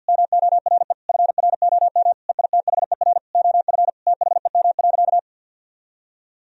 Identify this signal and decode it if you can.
Morse code, heard as a keyed tone, about 50 wpm, 700 Hz. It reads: MORE FROM EITHER OF THEM.
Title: Sentences from Top 1000 Words - Review Part 4 - 50wpm